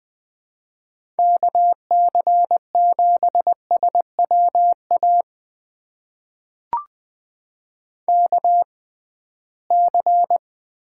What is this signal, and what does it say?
Morse code, heard as a keyed tone, about 20 wpm, spaced as for 15 wpm, 700 Hz. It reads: KC7SWA KC
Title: USA Call Signs - 6 Characters - 6x Character Spacing - 20wpm